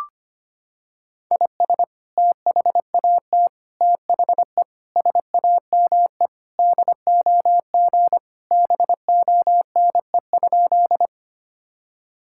Morse code, heard as a keyed tone, about 25 wpm, 700 Hz. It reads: IS THAT THE SAME DOG BONE?
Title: Sentences from Top 1600 Words 25wpm